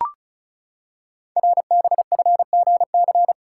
Morse code, heard as a keyed tone, about 36 wpm, 690 Hz. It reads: RBFGC
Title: Code Groups - 5 Letters - 35wpm